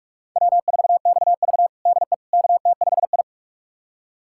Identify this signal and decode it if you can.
Morse code, heard as a keyed tone, about 41 wpm, 700 Hz. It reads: W4XV DE KT5I
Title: USA Callsign DE Callsign - 4 Characters - 45wpm